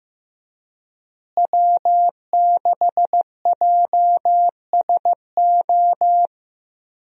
Morse code, heard as a keyed tone, about 15 wpm, 700 Hz. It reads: W6JSO
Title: USA Call Signs - 5 Characters - Repeated 3x - 15wpm